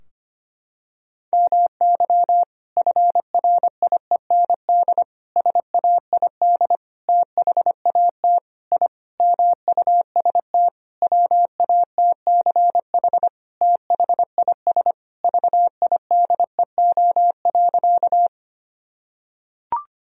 Morse code, heard as a keyed tone, about 25 wpm, 700 Hz. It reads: MY FRIEND SAID THAT I MUST WATCH THIS VIDEO.
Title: Sentences from Top 1000 Words - Review Part 4 - 25wpm